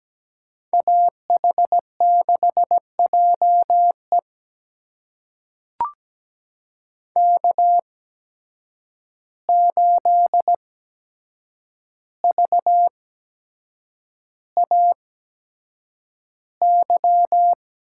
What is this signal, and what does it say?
Morse code, heard as a keyed tone, about 17 wpm, spaced as for 7 wpm, 700 Hz. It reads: AH6JE K8VAY
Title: USA Call Signs - 5 Characters - 8x Character Spacing - 17wpm